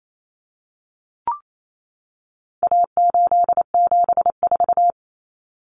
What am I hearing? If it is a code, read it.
Morse code, heard as a keyed tone, about 28 wpm, 700 Hz. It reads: A874
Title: Common Cut Numbers - 4-Digit 28wpm